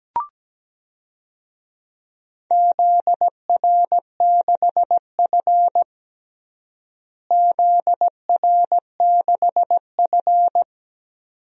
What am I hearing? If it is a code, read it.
Morse code, heard as a keyed tone, about 17 wpm, 700 Hz. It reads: ZR6F ZR6F
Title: DXCC-100 Call Signs - 4 Characters - Repeated 2x - 17wpm